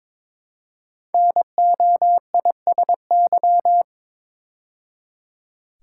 Morse code, heard as a keyed tone, about 22 wpm, 700 Hz. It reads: NOISY
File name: 5 Letter Words - First 3 Letters Given - 22wpm